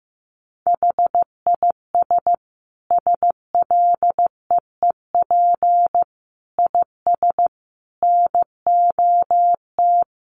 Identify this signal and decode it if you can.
Morse code, heard as a keyed tone, about 15 wpm, 700 Hz. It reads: HIS SLEEP IS NOT
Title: Sentences from Top 700 Words 15wpm